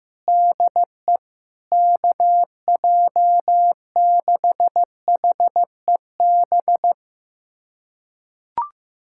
Morse code, heard as a keyed tone, about 16 wpm, 700 Hz. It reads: DE KJ6HEB
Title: USA Callsign DE Callsign 15wpm